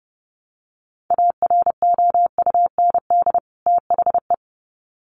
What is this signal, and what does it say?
Morse code, heard as a keyed tone, about 30 wpm, 700 Hz. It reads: AROUND THE
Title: Sets of 2 Words 30wpm